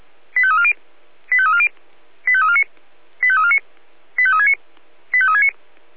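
Five-tone selective call (ZVEI), heard as quick runs of five tones, tones of about 70 ms. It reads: 85340 85340 85349 85349 85369 85369
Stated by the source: ZVEI five-tone selective call received by SDR.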